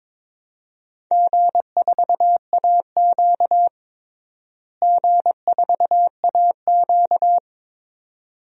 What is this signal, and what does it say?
Morse code, heard as a keyed tone, about 22 wpm, 700 Hz. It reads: G4AQ G4AQ
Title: DXCC-100 Call Signs - 4 Characters - Repeated 2x - 22wpm